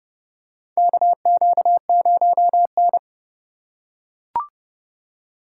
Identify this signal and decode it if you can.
Morse code, heard as a keyed tone, about 30 wpm, 700 Hz. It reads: KQ0N
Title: USA Call Signs - 4x Character Spacing - 30wpm